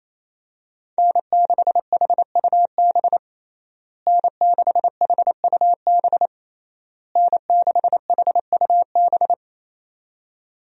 Morse code, heard as a keyed tone, about 28 wpm, 700 Hz. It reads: N6HUB N6HUB N6HUB
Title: USA Call Signs - 5 Characters - Repeated 3x - 28wpm